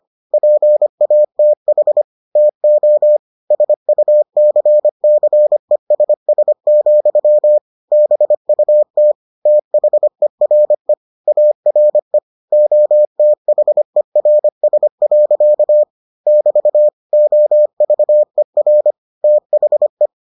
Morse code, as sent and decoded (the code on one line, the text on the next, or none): .--. .- - .... / - --- / ... ..- -.-. -.-. ... ... --..-- / -... ..- - / - .... .-. / .- .-. / --- - .... .-. ... .-.-.- / -...- / --- ...- . .-. / - .... .
PATH TO SUCCESS, BUT THERE ARE OTHERS. <BT> OVER THE